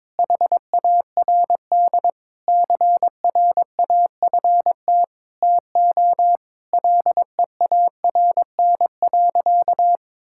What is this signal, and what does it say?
Morse code, heard as a keyed tone, about 22 wpm, 700 Hz. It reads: HARD CRAFT TO LEARN.
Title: Sentences from Top 1000 Words - Review Part 6 - 22wpm